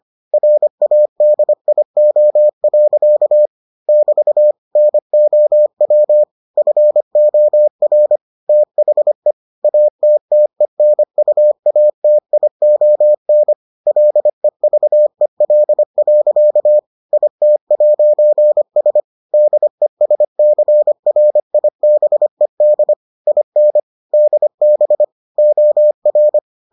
Morse code, heard as a keyed tone, about 25 wpm, 600 Hz. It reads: RADIO. <BT> NOW FOR THE ATTENUATION LEVEL. IT'S DESCRIBED IN DB OR